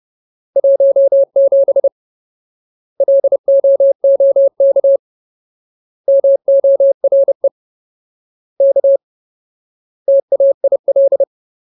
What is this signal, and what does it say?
Morse code, heard as a keyed tone, about 30 wpm, 550 Hz. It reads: 17 LOOK MORE K TAIL
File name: Ultimate Word List / 30WPM Ultimate